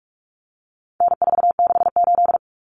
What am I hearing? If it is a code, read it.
Morse code, heard as a keyed tone, about 44 wpm, 700 Hz. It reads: N468
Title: Common Cut Numbers - 4-Digit 45wpm